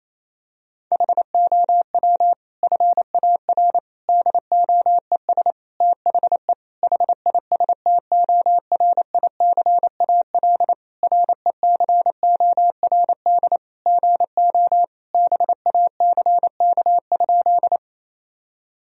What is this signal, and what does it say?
Morse code, heard as a keyed tone, about 28 wpm, 700 Hz. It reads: HOW FAR DOES THE HISTORICAL RECORD GO BACK?
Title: Sentences from Top 1000 Words - Review Part 4 - 28wpm